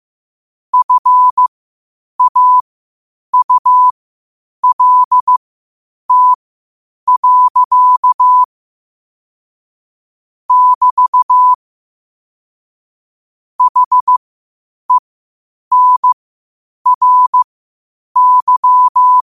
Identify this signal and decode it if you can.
Morse code, heard as a keyed tone, about 15 wpm, spaced as for 8 wpm, 1000 Hz. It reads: FAULT. <BT> HENRY
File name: qotdfiles / 8wpm